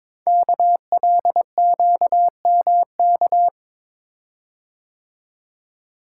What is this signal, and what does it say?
Morse code, heard as a keyed tone, about 22 wpm, 700 Hz. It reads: KLQMK
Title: Code Groups - 5 Letters - 22wpm